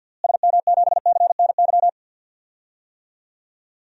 Morse code, heard as a keyed tone, about 50 wpm, 700 Hz. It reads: SM7CNY
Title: DXCC-100 Call Signs - 6 Characters - Repeated 3x - 50wpm